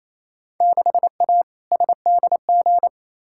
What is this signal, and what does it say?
Morse code, heard as a keyed tone, about 28 wpm, 700 Hz. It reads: 6A SDG
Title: Field Day Exchanges Encore - 28wpm